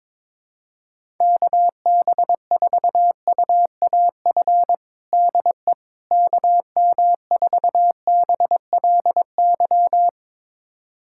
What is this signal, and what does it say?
Morse code, heard as a keyed tone, about 22 wpm, 700 Hz. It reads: KB4UAF DE KM4BLY
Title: USA Callsign DE Callsign 22wpm